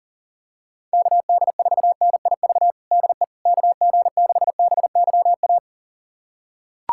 Morse code, heard as a keyed tone, about 37 wpm, 700 Hz. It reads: KD4NIV DE KG6BYA
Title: USA Callsign DE Callsign 40wpm